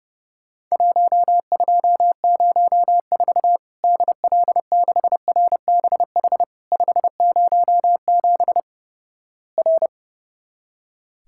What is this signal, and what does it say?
Morse code, heard as a keyed tone, about 30 wpm, 700 Hz. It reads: 1204 DL6RBH 507 R
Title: Worked All Europe DX Contest - QTC Exchange - 30wpm